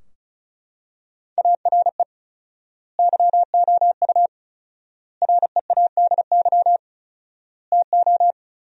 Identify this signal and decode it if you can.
Morse code, heard as a keyed tone, about 34 wpm, 700 Hz. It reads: ARE YOU READY TO
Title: 4x Word-Spacing - Sentences from Top 400 Words - 35wpm